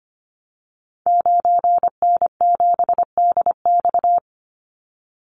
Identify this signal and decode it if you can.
Morse code, heard as a keyed tone, about 25 wpm, 700 Hz. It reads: 9N7DX
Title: Call Signs Repeated 2x - 25wpm